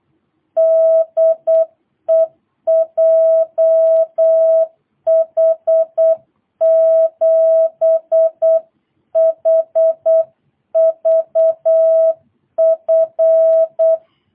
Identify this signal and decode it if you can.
Morse code, heard as a keyed tone, about 8 wpm, 660 Hz. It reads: DE JH7HVF
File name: jh8cbh / 2022cw